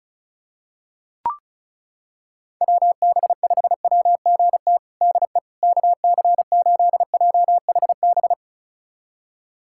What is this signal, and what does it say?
Morse code, heard as a keyed tone, about 34 wpm, 700 Hz. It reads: WB5WGT DE KC8JHB
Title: USA Callsign DE Callsign 35wpm